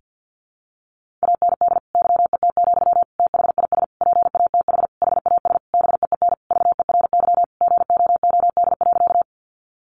Morse code, heard as a keyed tone, about 48 wpm, 700 Hz. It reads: AND YET, THIS PATH HAS BEEN VERY GOOD.
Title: Sentences from Top 1400 Words 50wpm